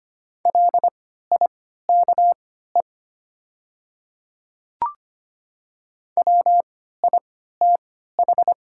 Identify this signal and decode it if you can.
Morse code, heard as a keyed tone, about 25 wpm, spaced as for 14 wpm, 690 Hz. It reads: LIKE WITH
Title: 3x Character Spacing - Top 200 Words - 25wpm